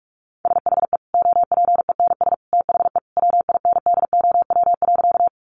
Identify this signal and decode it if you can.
Morse code, heard as a keyed tone, about 44 wpm, 700 Hz. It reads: SHE OPENS THE WINDOW.
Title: Easy Phrases 45wpm